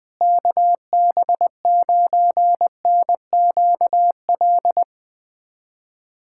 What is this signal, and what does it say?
Morse code, heard as a keyed tone, about 21 wpm, 700 Hz. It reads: KB9NQL
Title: USA Callsign DE Callsign - 6 Characters - 20wpm